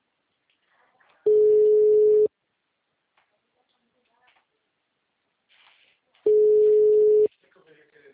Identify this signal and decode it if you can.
Ringback tone (Europe, 425 Hz), with speech.
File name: out- -2006- - - .19746